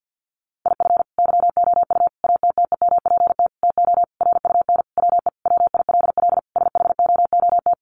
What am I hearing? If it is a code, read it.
Morse code, heard as a keyed tone, about 49 wpm, 700 Hz. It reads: IF YOU ATTEMPT TO RUN WE WILL SHOOT
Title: Sentences from Top 1200 Words 50wpm